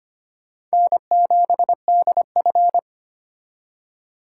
Morse code, heard as a keyed tone, about 31 wpm, 700 Hz. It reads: N7DF
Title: Speed Racing - USA 1x2 Call Signs 25wpm